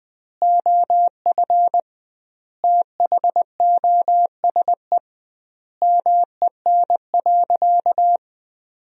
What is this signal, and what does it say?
Morse code, heard as a keyed tone, about 20 wpm, 700 Hz. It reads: OF THOSE MEN.
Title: 2x Word-Spacing - Sentences from Top 300 Words - 20wpm